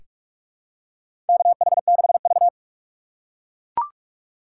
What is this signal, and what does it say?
Morse code, heard as a keyed tone, about 42 wpm, 700 Hz. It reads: KH6V
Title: USA Call Signs - 4 Characters - 4x Character Spacing - 45wpm